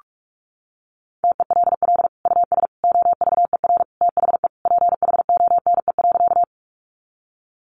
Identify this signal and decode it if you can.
Morse code, heard as a keyed tone, about 44 wpm, 700 Hz. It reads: TELL US OVER THE PHONE.
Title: Sentences from Top 1100 Words 45wpm